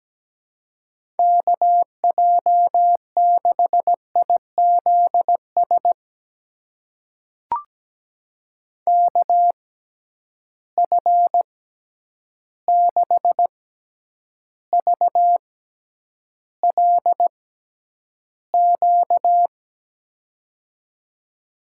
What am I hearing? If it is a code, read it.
Morse code, heard as a keyed tone, about 17 wpm, spaced as for 9 wpm, 700 Hz. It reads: KJ6IZS KF6VLQ
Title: USA Call Signs - 6 Characters - 6x Character Spacing - 17wpm